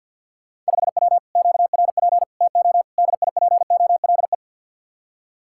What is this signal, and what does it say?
Morse code, heard as a keyed tone, about 66 wpm, spaced as for 57 wpm, 700 Hz. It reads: 5W QRP TO DIPOLE
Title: Speed-Racing Common QSO Phrases Encore - 50wpm